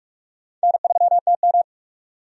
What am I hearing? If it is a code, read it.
Morse code, heard as a keyed tone, about 46 wpm, 700 Hz. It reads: N3TM
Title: USA Call Signs - 4 Characters - 45wpm